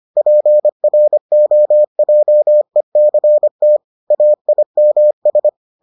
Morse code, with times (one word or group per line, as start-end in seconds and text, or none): 0.2-3.8 PROJECT
4.1-5.5 AIMS